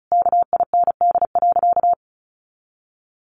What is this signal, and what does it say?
Morse code, heard as a keyed tone, about 34 wpm, 700 Hz. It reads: KIND.
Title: Sentences from Top 1500 Words 35wpm